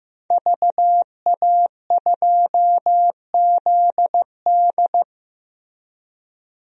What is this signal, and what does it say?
Morse code, heard as a keyed tone, about 15 wpm, 700 Hz. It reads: VA2ZD
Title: DXCC-100 Call Signs - 5 Characters - Repeated 3x - 15wpm